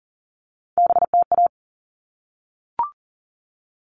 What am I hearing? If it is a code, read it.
Morse code, heard as a keyed tone, about 40 wpm, 700 Hz. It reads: BTA
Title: All Cut Numbers - 3-Digit 40wpm